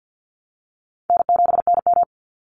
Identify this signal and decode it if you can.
Morse code, heard as a keyed tone, about 48 wpm, 700 Hz. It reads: N7NM
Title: Easy Call Signs Repeated 3x - 50wpm